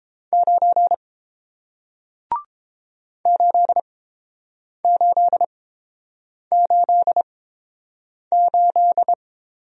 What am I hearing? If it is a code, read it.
Morse code, heard as a keyed tone, about 31 wpm, spaced as for 7 wpm, 700 Hz. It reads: 9 8888